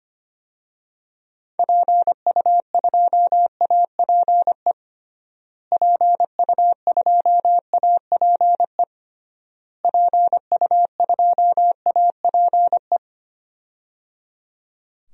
Morse code, heard as a keyed tone, about 25 wpm, 700 Hz. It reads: PU2APE PU2APE PU2APE
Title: DXCC-100 Call Signs - 6 Characters - Repeated 3x - 25wpm